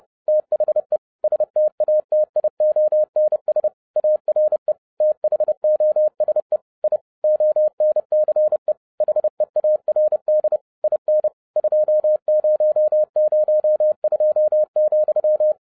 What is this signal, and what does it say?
Morse code, heard as a keyed tone, about 30 wpm, 610 Hz. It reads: THE STATIONS ARE THOSE I ONCE HEARD IN 2002,
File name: 2012-01-mp3 / n12-I DIDNT MISS THE SOLAR CYCLE 24-30wpm